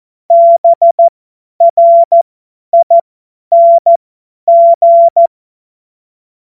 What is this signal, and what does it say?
Morse code, heard as a keyed tone, about 14 wpm, spaced as for 10 wpm, 680 Hz. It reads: BRING
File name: Morsekurs / Witze / Witz 02. Dezember 2025 50 F70